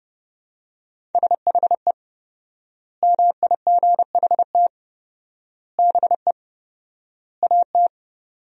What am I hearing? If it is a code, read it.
Morse code, heard as a keyed tone, about 32 wpm, 700 Hz. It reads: SHE MIGHT BE AT
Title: 4x Word-Spacing - Sentences from Top 300 Words - 30wpm